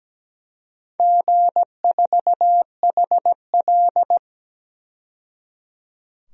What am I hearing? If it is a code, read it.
Morse code, heard as a keyed tone, about 17 wpm, 700 Hz. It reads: G4HL
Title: DXCC-100 Call Signs - 4 Characters - 17wpm